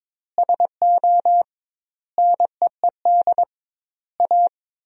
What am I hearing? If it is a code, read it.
Morse code, heard as a keyed tone, about 22 wpm, 700 Hz. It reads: SO NEED A
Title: 2x Word-Spacing - Sentences from Top 200 Words - 22wpm